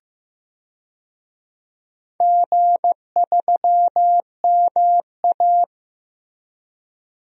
Morse code, heard as a keyed tone, about 15 wpm, 700 Hz. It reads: G3MA
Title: DXCC-100 Call Signs - 4 Characters - Repeated 3x - 15wpm